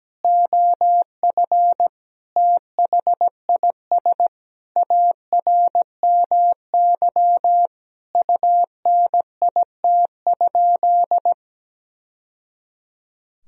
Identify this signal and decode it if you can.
Morse code, heard as a keyed tone, about 17 wpm, 700 Hz. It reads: OF THIS ARMY UNIT?
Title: Sentences from Top 1000 Words - Review Part 2 - 17wpm